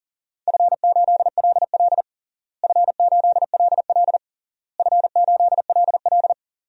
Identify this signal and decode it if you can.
Morse code, heard as a keyed tone, about 40 wpm, 700 Hz. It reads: F8LL F8LL F8LL
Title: DXCC-100 Call Signs - 4 Characters - Repeated 3x - 40wpm